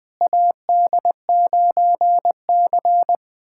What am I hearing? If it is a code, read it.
Morse code, heard as a keyed tone, about 20 wpm, 700 Hz. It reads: AD9C
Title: USA Callsign DE Callsign - 4 Characters - 20wpm